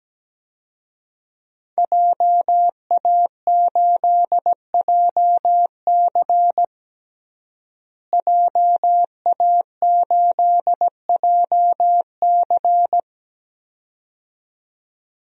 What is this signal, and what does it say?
Morse code, heard as a keyed tone, about 17 wpm, 700 Hz. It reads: JA8JC JA8JC
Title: DXCC-100 Call Signs - 5 Characters - Repeated 2x - 17wpm